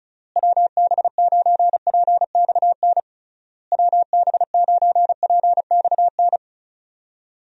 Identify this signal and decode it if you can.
Morse code, heard as a keyed tone, about 36 wpm, 700 Hz. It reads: WB9PXN WB9PXN
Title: USA Call Signs - 6 Characters - Repeated 3x - 35wpm